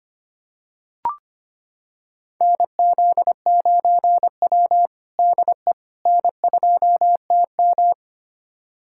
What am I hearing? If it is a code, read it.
Morse code, heard as a keyed tone, about 25 wpm, 700 Hz. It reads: NZ9W DE N2TM
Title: USA Callsign DE Callsign - 4 Characters - 25wpm